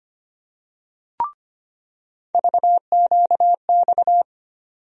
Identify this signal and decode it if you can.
Morse code, heard as a keyed tone, about 25 wpm, 700 Hz. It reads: VQX
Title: USA Call Sign Postfixes - 3 Letters 25wpm